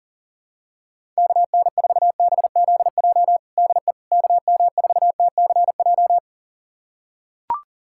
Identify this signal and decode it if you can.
Morse code, heard as a keyed tone, about 40 wpm, 700 Hz. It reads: KN4BZJ DE KM4TCJ
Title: USA Callsign DE Callsign - 6 Characters - 40wpm